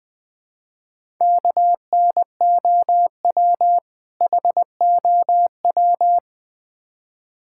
Morse code, heard as a keyed tone, about 20 wpm, 700 Hz. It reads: KNOW HOW
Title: Build A Phrase - 3-Word Phrases - 20wpm